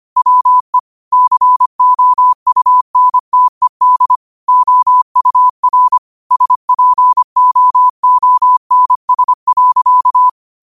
Morse code, heard as a keyed tone, about 25 wpm, 1000 Hz. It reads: WE COUNTED OUR SPOONS.
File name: qotdfiles / 25wpm